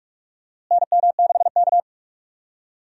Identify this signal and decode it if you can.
Morse code, heard as a keyed tone, about 42 wpm, 700 Hz. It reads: NM6K
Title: USA Call Signs - 4 Characters - Repeated 2x - 45wpm